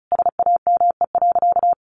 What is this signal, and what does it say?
Morse code, heard as a keyed tone, about 34 wpm, 700 Hz. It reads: SAME.
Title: Sentences from Top 300 Words 35wpm